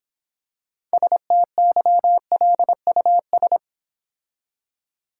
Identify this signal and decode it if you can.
Morse code, heard as a keyed tone, about 26 wpm, 700 Hz. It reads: STYLUS